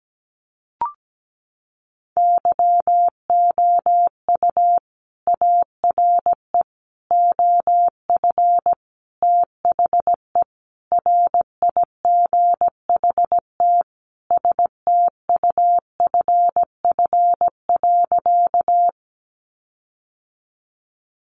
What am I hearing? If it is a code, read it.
Morse code, heard as a keyed tone, about 17 wpm, 700 Hz. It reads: YOU ARE OF THE RIGHT STUFF.
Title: Sentences from Top 1400 Words 17wpm